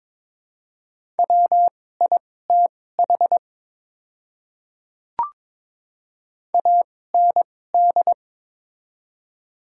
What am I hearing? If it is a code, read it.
Morse code, heard as a keyed tone, about 21 wpm, spaced as for 16 wpm, 700 Hz. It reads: WITH AND
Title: 2x Character Spacing - Top 100 Words - 22wpm